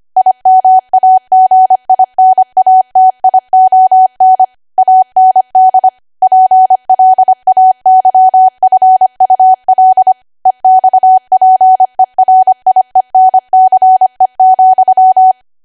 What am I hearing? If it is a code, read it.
Morse code, heard as a keyed tone, about 25 wpm, 750 Hz. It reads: IMAGINATION AND PLAYFUL EXPERIENCE,
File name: Archive / 25 WPM / 25WPM